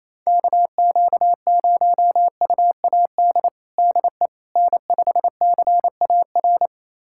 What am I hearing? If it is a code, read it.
Morse code, heard as a keyed tone, about 28 wpm, 700 Hz. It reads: KQ0UAD DE N5CAR
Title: USA Callsign DE Callsign 28wpm